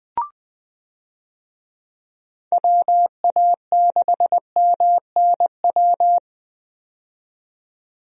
Morse code, heard as a keyed tone, about 20 wpm, 700 Hz. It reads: WA6MNW
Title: USA Call Signs - 6 Characters - 20wpm